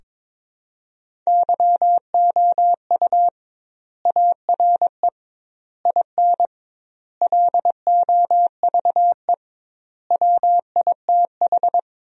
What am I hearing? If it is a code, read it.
Morse code, heard as a keyed tone, about 22 wpm, 700 Hz. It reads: YOU ARE IN LOVE WITH
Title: 2x Word-Spacing - Sentences from Top 400 Words - 22wpm